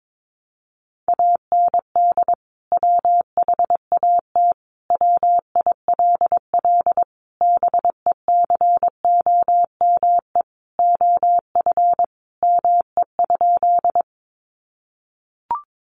Morse code, heard as a keyed tone, about 22 wpm, 700 Hz. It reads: AND WHAT WILL BECOME OF ME?
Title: Sentences from Top 600 Words 22wpm